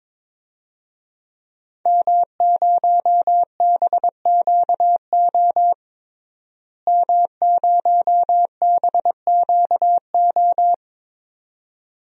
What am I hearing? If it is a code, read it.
Morse code, heard as a keyed tone, about 22 wpm, 700 Hz. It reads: M0BQO M0BQO
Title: DXCC-100 Call Signs - 5 Characters - Repeated 2x - 22wpm